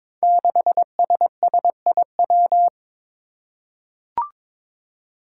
Morse code, heard as a keyed tone, about 22 wpm, 700 Hz. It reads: 6SSIW
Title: Code Groups - 5 Letter:Numbers - 22wpm